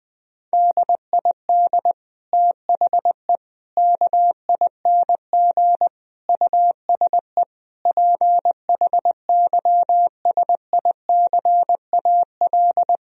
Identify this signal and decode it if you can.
Morse code, heard as a keyed tone, about 20 wpm, 700 Hz. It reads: DID THE KING USE PHYSICAL
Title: Sentences from Top 1000 Words - Review Part 4 - 20wpm